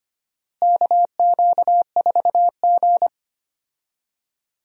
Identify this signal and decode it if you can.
Morse code, heard as a keyed tone, about 25 wpm, 700 Hz. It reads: KQ4G
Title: USA Call Signs - 4 Characters - Repeated 2x - 25wpm